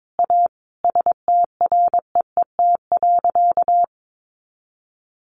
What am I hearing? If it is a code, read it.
Morse code, heard as a keyed tone, about 22 wpm, 700 Hz. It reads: A STREET.
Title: Sentences from Top 500 Words 22wpm